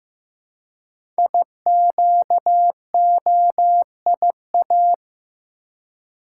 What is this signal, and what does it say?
Morse code, heard as a keyed tone, about 15 wpm, 700 Hz. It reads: IQOIA
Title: Code Groups - 5 Letters - 15wpm